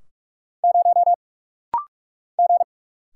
Morse code, heard as a keyed tone, about 44 wpm, 690 Hz. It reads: 0 G
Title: Single Letter-Number - Rapid-Fire 45wpm